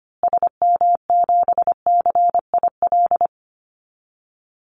Morse code, heard as a keyed tone, about 25 wpm, 700 Hz. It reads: SM7CIL
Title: Call Signs Repeated 2x - 25wpm